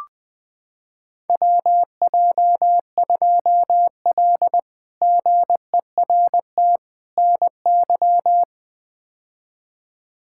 Morse code, heard as a keyed tone, about 21 wpm, 700 Hz. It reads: WJ2L GERT NY
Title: K1USN SST QSO Practice - 20wpm